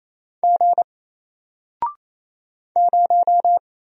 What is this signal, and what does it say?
Morse code, heard as a keyed tone, about 28 wpm, 700 Hz. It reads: G 0
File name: Single-Letter-Number-28wpm